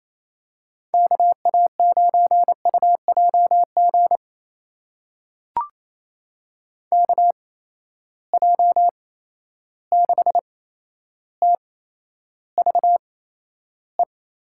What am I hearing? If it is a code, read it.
Morse code, heard as a keyed tone, about 28 wpm, spaced as for 12 wpm, 700 Hz. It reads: KA9UJG KJ6TVE
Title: USA Call Signs - 6 Characters - 8x Character Spacing - 28wpm